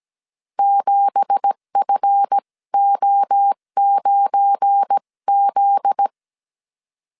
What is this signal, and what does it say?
Morse code, heard as a keyed tone, about 17 wpm, 800 Hz. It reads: 7FO9Z
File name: CW / DET / 1 9 1500C S B 16wpm 2856